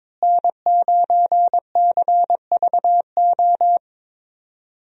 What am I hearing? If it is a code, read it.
Morse code, heard as a keyed tone, about 22 wpm, 700 Hz. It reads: N9CVO
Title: USA Call Signs - 5 Characters - Repeated 3x - 22wpm